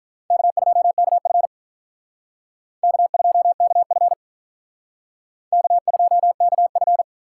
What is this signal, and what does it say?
Morse code, heard as a keyed tone, about 44 wpm, 700 Hz. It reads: K2KF K2KF K2KF